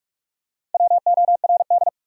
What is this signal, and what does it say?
Morse code, heard as a keyed tone, about 43 wpm, 700 Hz. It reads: WORD
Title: Top 100 Words Repeated 45wpm